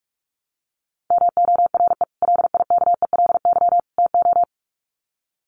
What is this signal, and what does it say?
Morse code, heard as a keyed tone, about 44 wpm, 700 Hz. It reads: MORE LIKELY TO
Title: Sets of 3 Words 45wpm